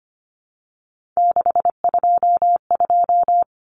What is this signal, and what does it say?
Morse code, heard as a keyed tone, about 25 wpm, 700 Hz. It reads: 622
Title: Common Cut Numbers - 3-Digit 25wpm